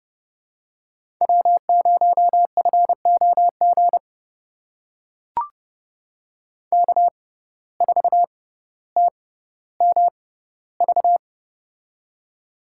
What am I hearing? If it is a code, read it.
Morse code, heard as a keyed tone, about 30 wpm, spaced as for 15 wpm, 700 Hz. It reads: W0FOG K4TMV
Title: USA Call Signs - 5 Characters - 6x Character Spacing - 30wpm